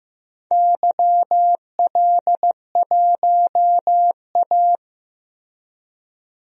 Morse code, heard as a keyed tone, about 15 wpm, 700 Hz. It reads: YL1A
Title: DXCC-100 Call Signs - 4 Characters - Repeated 2x - 15wpm